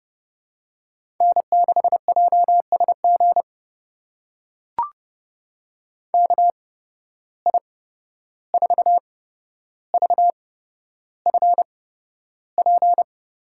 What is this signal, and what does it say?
Morse code, heard as a keyed tone, about 32 wpm, spaced as for 12 wpm, 700 Hz. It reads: N6JSG KI4VFP
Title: USA Call Signs - 8x Character Spacing - 30wpm